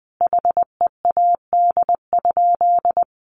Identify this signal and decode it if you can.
Morse code, heard as a keyed tone, about 20 wpm, 700 Hz. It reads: HEAD?
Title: Sentences from Top 600 Words 20wpm